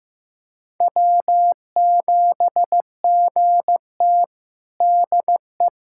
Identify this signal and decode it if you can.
Morse code, heard as a keyed tone, about 15 wpm, 700 Hz. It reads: W7GT DE
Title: USA Callsign DE Callsign - 4 Characters - 15wpm